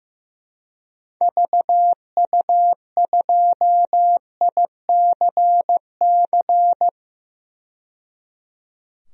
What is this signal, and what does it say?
Morse code, heard as a keyed tone, about 15 wpm, 700 Hz. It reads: VU2ICC
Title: DXCC-100 Call Signs - 6 Characters - 15wpm